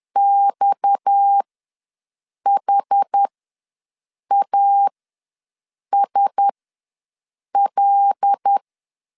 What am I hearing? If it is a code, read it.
Morse code, heard as a keyed tone, about 10 wpm, spaced as for 6 wpm, 800 Hz. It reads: XHASL
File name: LEZIONI / 6 PKX0 / 6 PKX ws 10wpm 3706